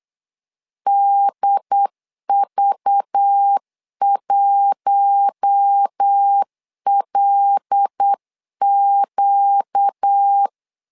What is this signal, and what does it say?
Morse code, heard as a keyed tone, about 8 wpm, 800 Hz. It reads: DV1LQ
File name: LEZIONI / 5 LBVC / 5 LBV 8wpm 2837